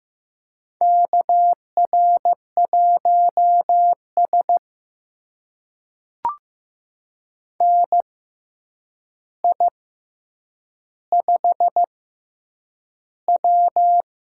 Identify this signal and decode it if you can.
Morse code, heard as a keyed tone, about 15 wpm, spaced as for 8 wpm, 700 Hz. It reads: KR1S NI5W
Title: USA Call Signs - 4 Characters - 6x Character Spacing - 15wpm